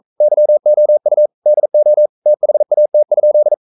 Morse code, heard as a keyed tone, about 42 wpm, 600 Hz. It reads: YOU DO THAT?
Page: Romanen Never af Ken Follet på engelsk.